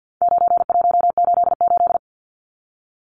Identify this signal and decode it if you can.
Morse code, heard as a keyed tone, about 50 wpm, 700 Hz. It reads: 9188
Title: Numbers - 4-Digit 50wpm